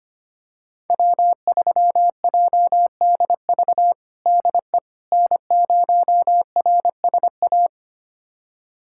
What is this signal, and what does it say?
Morse code, heard as a keyed tone, about 25 wpm, 700 Hz. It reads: W3JDV DE N0RSA
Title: USA Callsign DE Callsign - 5 Characters - 25wpm